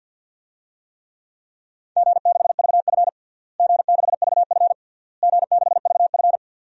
Morse code, heard as a keyed tone, about 52 wpm, 700 Hz. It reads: G6VF G6VF G6VF
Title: DXCC-100 Call Signs - 4 Characters - Repeated 3x - 50wpm